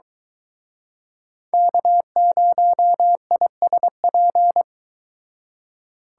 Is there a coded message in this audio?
Yes: Morse code from 1.5 to 4.6 s, heard as a keyed tone, about 23 wpm, 700 Hz. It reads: K0ISP